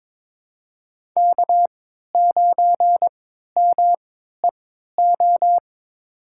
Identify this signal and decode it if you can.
Morse code, heard as a keyed tone, about 22 wpm, spaced as for 13 wpm, 700 Hz. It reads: K9MEO
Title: USA Call Signs - 5 Characters - 3x Character Spacing - 22wpm